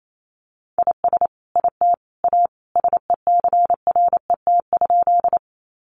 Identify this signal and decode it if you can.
Morse code, heard as a keyed tone, about 28 wpm, 700 Hz. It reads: IS IT A SECRET?
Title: Sentences from Top 1200 Words 28wpm